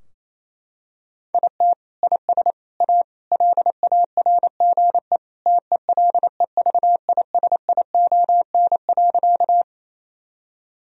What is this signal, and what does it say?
Morse code, heard as a keyed tone, about 28 wpm, 700 Hz. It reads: IT IS A LARGE TELEVISION.
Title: Sentences from Top 1000 Words - Review Part 4 - 28wpm